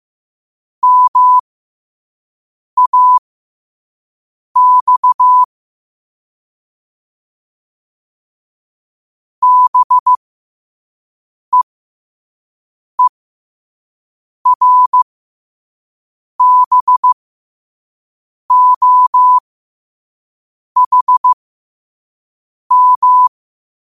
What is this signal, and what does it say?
Morse code, heard as a keyed tone, about 15 wpm, spaced as for 5 wpm, 1000 Hz. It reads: MAX BEERBOHM
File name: qotdfiles / 5wpm